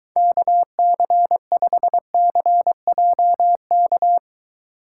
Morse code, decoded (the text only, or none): KC5CJK